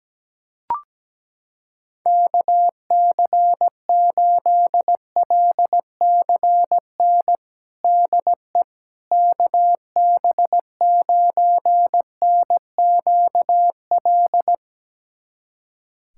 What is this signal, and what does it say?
Morse code, heard as a keyed tone, about 17 wpm, 700 Hz. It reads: KC8LCN DE KB9NQL
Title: USA Callsign DE Callsign - 6 Characters - 17wpm